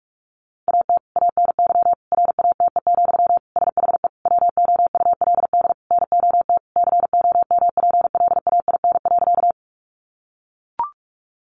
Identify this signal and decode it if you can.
Morse code, heard as a keyed tone, about 44 wpm, 700 Hz. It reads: AT ANY RATE, SHE WOULD NOT COMPLAIN.
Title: Sentences from Top 1300 Words 45wpm